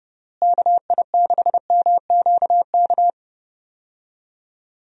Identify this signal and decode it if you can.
Morse code, heard as a keyed tone, about 30 wpm, 700 Hz. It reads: KI6MQK
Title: USA Call Signs - 6 Characters - Repeated 3x - 30wpm